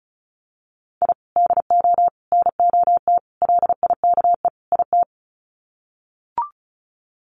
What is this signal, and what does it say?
Morse code, heard as a keyed tone, about 34 wpm, 700 Hz. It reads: I DO NOT LIKE IT
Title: Sets of 5 Words Encore 35wpm